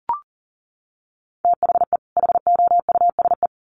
Morse code, heard as a keyed tone, about 40 wpm, 700 Hz. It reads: THE HOUSE
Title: Sets of 2 Words 40wpm